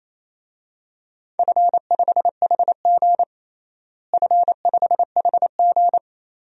Morse code, heard as a keyed tone, about 28 wpm, 700 Hz. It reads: F5HG F5HG
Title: DXCC-100 Call Signs - 4 Characters - Repeated 3x - 28wpm